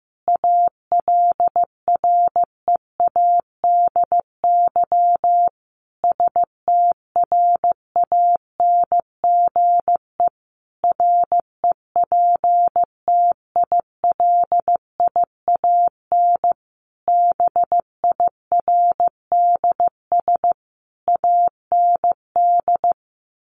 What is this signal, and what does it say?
Morse code, heard as a keyed tone, about 14 wpm, 700 Hz. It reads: ALREADY STRANGE REPTILIAN BIRDS AND
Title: A Sound Of Thunder 15wpm